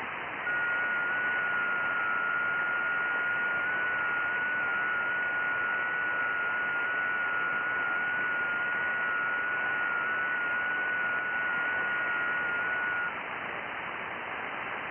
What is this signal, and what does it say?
FT8.
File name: IC-7100